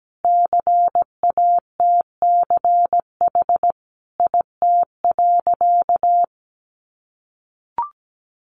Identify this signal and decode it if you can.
Morse code, heard as a keyed tone, about 17 wpm, 700 Hz. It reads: CATCH IT.
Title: Additional Easy Phrases 17wpm